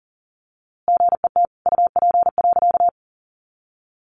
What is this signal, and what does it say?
Morse code, heard as a keyed tone, about 40 wpm, 700 Hz. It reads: GET UP.
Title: A Sound Of Thunder 40wpm